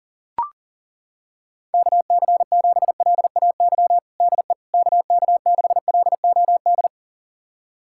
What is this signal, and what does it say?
Morse code, heard as a keyed tone, about 40 wpm, 700 Hz. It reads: KC7LAY DE KK6LOD
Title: USA Callsign DE Callsign - 6 Characters - 40wpm